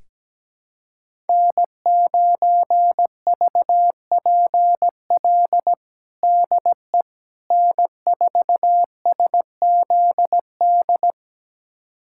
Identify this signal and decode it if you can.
Morse code, heard as a keyed tone, about 17 wpm, 700 Hz. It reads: N9VPL DE N4SZD
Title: USA Callsign DE Callsign - 5 Characters - 17wpm